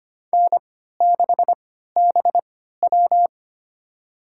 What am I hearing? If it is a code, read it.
Morse code, heard as a keyed tone, about 25 wpm, spaced as for 14 wpm, 700 Hz. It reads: N6BW
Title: USA Call Signs - 4 Characters - 3x Character Spacing - 25wpm